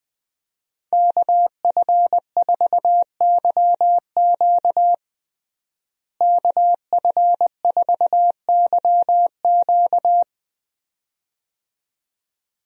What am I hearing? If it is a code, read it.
Morse code, heard as a keyed tone, about 20 wpm, 700 Hz. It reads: KF4YQ KF4YQ
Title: USA Call Signs - 5 Characters - Repeated 2x - 20wpm